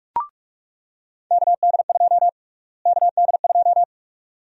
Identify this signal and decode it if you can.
Morse code, heard as a keyed tone, about 45 wpm, 700 Hz. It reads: KD2 KD2
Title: USA Call Sign Prefixes - Repeated 2x - 45wpm